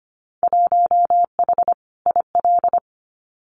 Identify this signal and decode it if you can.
Morse code, heard as a keyed tone, about 25 wpm, 700 Hz. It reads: 1H IL
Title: Winter Field Day Exchanges 25wpm